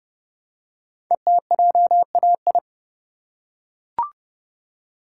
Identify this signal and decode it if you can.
Morse code, heard as a keyed tone, about 30 wpm, 700 Hz. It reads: ETJAI
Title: Code Groups - 5 Letters - 30wpm